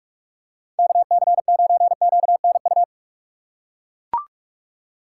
Morse code, heard as a keyed tone, about 42 wpm, 700 Hz. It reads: KC9QNU
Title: USA Call Signs - 6 Characters - 3x Character Spacing - 45wpm